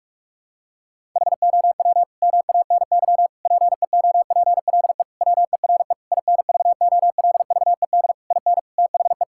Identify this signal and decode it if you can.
Morse code, heard as a keyed tone, about 41 wpm, 700 Hz. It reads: HOW MANY PEOPLE WERE INVOLVED IN THE
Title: Sentences from Top 1000 Words - Review Part 5 - 45wpm